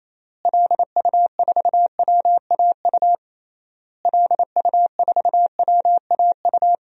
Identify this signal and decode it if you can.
Morse code, heard as a keyed tone, about 28 wpm, 700 Hz. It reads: LU4WAU LU4WAU
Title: DXCC-100 Call Signs - 6 Characters - Repeated 3x - 28wpm